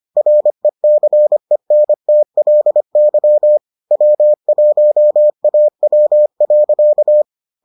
Morse code, heard as a keyed tone, about 25 wpm, 600 Hz. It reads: RECENTLY W1AW.